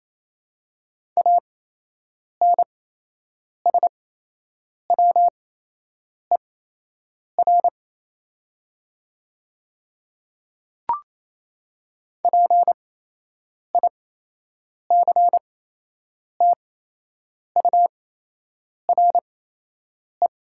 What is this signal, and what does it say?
Morse code, heard as a keyed tone, about 28 wpm, spaced as for 8 wpm, 690 Hz. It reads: ANSWER PICTURE
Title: 8x Character Spacing - Top 300 Words - 28wpm